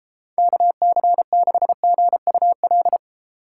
Morse code, heard as a keyed tone, about 32 wpm, 700 Hz. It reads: KC6GUL